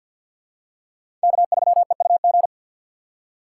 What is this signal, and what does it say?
Morse code, heard as a keyed tone, about 47 wpm, 700 Hz. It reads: K3EUG
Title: USA Call Signs - 4x Character Spacing - 50wpm